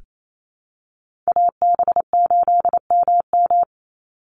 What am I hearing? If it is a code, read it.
Morse code, heard as a keyed tone, about 28 wpm, 700 Hz. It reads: AB8MM
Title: USA Call Signs 28wpm